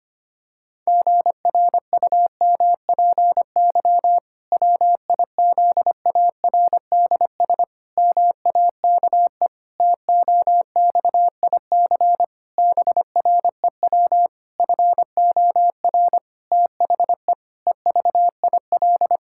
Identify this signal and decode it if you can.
Morse code, heard as a keyed tone, about 25 wpm, 700 Hz. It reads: GRUMPY WIZARDS MAKE TOXIC BREW FOR THE EVIL
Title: Pangram Challenge v2 - 25wpm